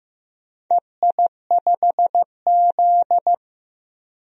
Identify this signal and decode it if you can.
Morse code, heard as a keyed tone, about 15 wpm, 700 Hz. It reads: EI5Z
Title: DXCC-100 Call Signs - 4 Characters - 15wpm